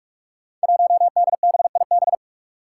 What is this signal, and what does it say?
Morse code, heard as a keyed tone, about 46 wpm, 700 Hz. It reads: 1DBIB
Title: Five Challenging Letter-Number Combinations 45wpm